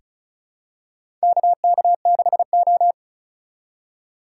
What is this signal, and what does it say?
Morse code, heard as a keyed tone, about 36 wpm, 700 Hz. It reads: KK6O
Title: USA Call Signs - 4 Characters - 2x Character Spacing - 35wpm